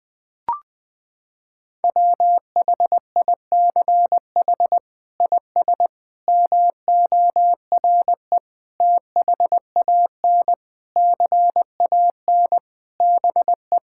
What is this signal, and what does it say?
Morse code, heard as a keyed tone, about 21 wpm, 700 Hz. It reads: WHICH IS MORE THAN CAN BE
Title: Sets of 8 Words - 20wpm